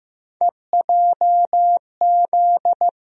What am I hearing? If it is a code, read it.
Morse code, heard as a keyed tone, about 15 wpm, 700 Hz. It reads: EJZ
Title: USA Call Sign Postfixes - 3 Letters 15wpm